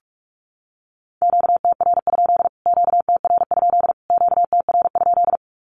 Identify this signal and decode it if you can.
Morse code, heard as a keyed tone, about 44 wpm, 700 Hz. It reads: QTR? QTR? QTR?
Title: Q-Codes Repeated 3x 45wpm